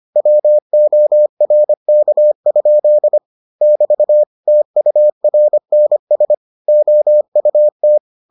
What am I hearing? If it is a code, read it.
Morse code, heard as a keyed tone, about 25 wpm, 600 Hz. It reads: WORK? <BT> TURNS OUT